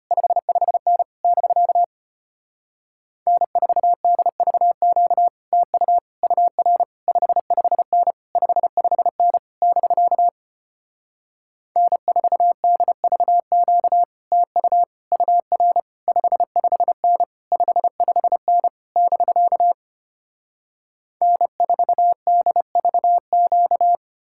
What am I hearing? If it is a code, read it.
Morse code, heard as a keyed tone, about 33 wpm, 700 Hz. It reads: 55N <BK> N4DVQ TU UR 55N 55N <BK> N4DVQ TU UR 55N 55N <BK> N4DVQ